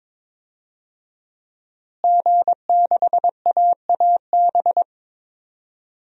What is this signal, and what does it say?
Morse code, heard as a keyed tone, about 22 wpm, 700 Hz. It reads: G6AAB
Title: DXCC-100 Call Signs - 5 Characters - Repeated 2x - 22wpm